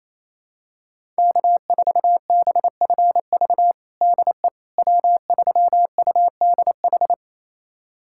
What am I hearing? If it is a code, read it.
Morse code, heard as a keyed tone, about 28 wpm, 700 Hz. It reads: K4BFV DE W3UDH
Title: USA Callsign DE Callsign - 5 Characters - 28wpm